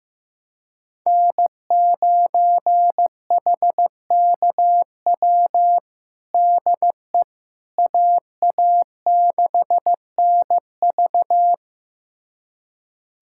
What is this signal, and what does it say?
Morse code, heard as a keyed tone, about 16 wpm, 700 Hz. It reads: N9HKW DE AA6NV
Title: USA Callsign DE Callsign - 5 Characters - 15wpm